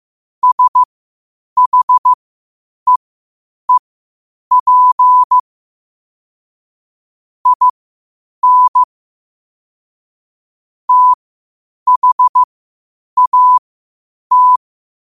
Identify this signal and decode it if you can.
Morse code, heard as a keyed tone, about 15 wpm, spaced as for 8 wpm, 1000 Hz. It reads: SHEEP IN THAT
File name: qotdfiles / 8wpm